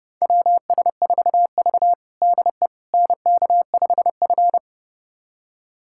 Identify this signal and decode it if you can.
Morse code, heard as a keyed tone, about 32 wpm, 700 Hz. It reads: WS4V DE NK5F
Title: USA Callsign DE Callsign - 4 Characters - 30wpm